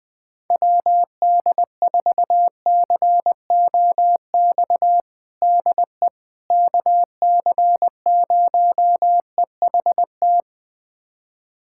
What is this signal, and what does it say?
Morse code, heard as a keyed tone, about 21 wpm, 700 Hz. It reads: WD4COX DE KC0EHT
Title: USA Callsign DE Callsign 20wpm